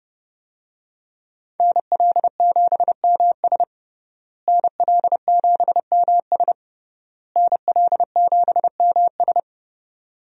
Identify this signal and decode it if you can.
Morse code, heard as a keyed tone, about 30 wpm, 700 Hz. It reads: NL7MS NL7MS NL7MS
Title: USA Call Signs - 5 Characters - Repeated 3x - 30wpm